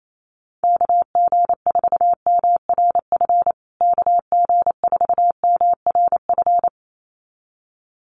Morse code, heard as a keyed tone, about 28 wpm, 700 Hz. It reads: KG4MRF KG4MRF
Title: USA Call Signs Repeated 2x 28wpm